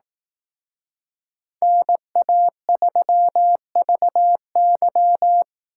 Morse code, heard as a keyed tone, about 18 wpm, 700 Hz. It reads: NA3VY
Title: Speed Racing - USA 2x2 Call Signs 15wpm